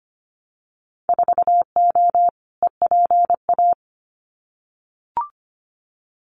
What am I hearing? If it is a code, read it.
Morse code, heard as a keyed tone, about 25 wpm, 690 Hz. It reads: 4O EPA
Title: Winter Field Day Exchanges 25wpm